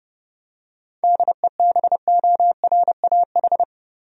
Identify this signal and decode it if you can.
Morse code, heard as a keyed tone, about 32 wpm, 700 Hz. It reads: DEBORAH
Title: USA Common Names - 30wpm